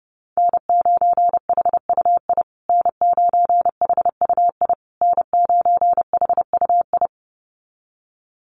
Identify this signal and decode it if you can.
Morse code, heard as a keyed tone, about 30 wpm, 700 Hz. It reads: N9HUI N9HUI N9HUI
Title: USA Call Signs Repeated 3x 30wpm